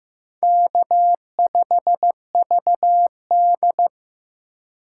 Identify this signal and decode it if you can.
Morse code, heard as a keyed tone, about 16 wpm, 700 Hz. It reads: K5VD
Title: USA Call Signs - 4 Characters - 3x Character Spacing - 15wpm